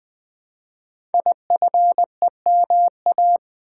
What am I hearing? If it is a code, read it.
Morse code, heard as a keyed tone, about 20 wpm, 700 Hz. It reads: IFEMA
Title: Code Groups - 5 Letter:Numbers - 20wpm